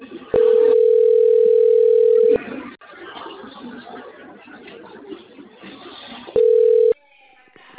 Ringback tone (North America), 440 + 480 Hz.
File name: out- - - - - .96914